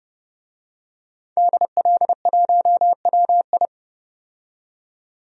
Morse code, heard as a keyed tone, about 30 wpm, 700 Hz. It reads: DL1WI
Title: DXCC-100 Call Signs - 5 Characters - 30wpm